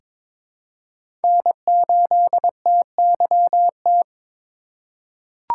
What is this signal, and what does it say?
Morse code, heard as a keyed tone, about 22 wpm, 700 Hz. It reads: N8TYT
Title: USA Call Signs - 5 Characters - 4x Character Spacing - 22wpm